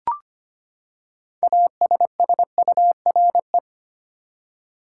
Morse code, heard as a keyed tone, about 25 wpm, 700 Hz. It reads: ASSURE
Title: Six Letter Words - 25wpm